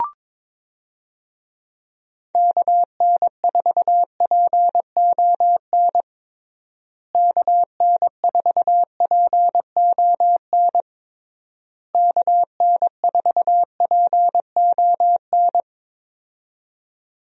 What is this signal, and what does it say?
Morse code, heard as a keyed tone, about 22 wpm, 700 Hz. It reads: KN4PON KN4PON KN4PON
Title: USA Call Signs - 6 Characters - Repeated 3x - 22wpm